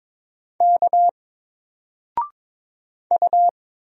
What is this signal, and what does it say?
Morse code, heard as a keyed tone, about 22 wpm, 700 Hz. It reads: K U
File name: Single-Letter-Number-22wpm